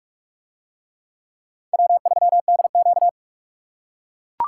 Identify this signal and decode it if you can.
Morse code, heard as a keyed tone, about 44 wpm, 700 Hz. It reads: W3DQ